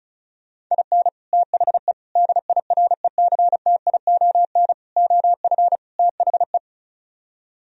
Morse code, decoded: IN THE DIRECTION OF THE

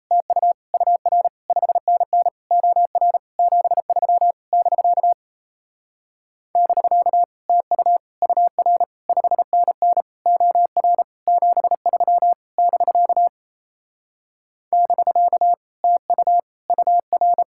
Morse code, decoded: TU UR 5NN OR 73 <BK> <BK> TU UR 5NN OR 73 <BK> <BK> TU UR

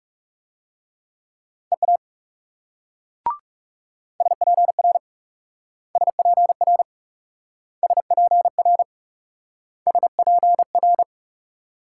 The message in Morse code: EA SPR SPR SPR SPR

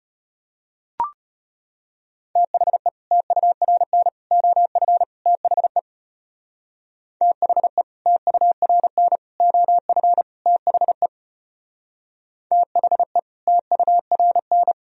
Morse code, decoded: THE TURN OF THE THE TURN OF THE THE TURN

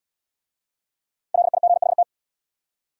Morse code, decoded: LIGHT

